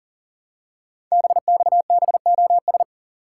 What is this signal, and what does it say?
Morse code, heard as a keyed tone, about 38 wpm, 700 Hz. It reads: BXBOS
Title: Code Groups - 5 Letters - 40wpm